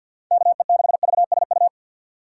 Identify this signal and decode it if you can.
Morse code, heard as a keyed tone, about 47 wpm, 700 Hz. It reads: KE6VSU
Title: USA Call Signs - 4x Character Spacing - 50wpm